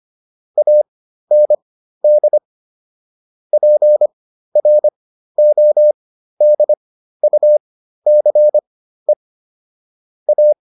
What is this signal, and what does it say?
Morse code, heard as a keyed tone, about 25 wpm, spaced as for 13 wpm, 600 Hz. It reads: AND PRODUCE A